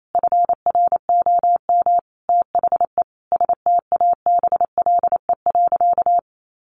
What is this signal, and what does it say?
Morse code, heard as a keyed tone, about 28 wpm, 700 Hz. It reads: FROM THE STABLE.
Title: Sentences from Top 1500 Words 28wpm